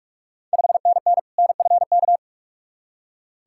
Morse code, heard as a keyed tone, about 46 wpm, 690 Hz. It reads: 5NN NFK
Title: Ontario QSO Party Exchange 45wpm